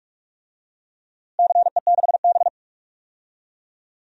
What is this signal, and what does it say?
Morse code, heard as a keyed tone, about 44 wpm, 700 Hz. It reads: CE6B
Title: DXCC-100 Call Signs - 4 Characters - Repeated 2x - 45wpm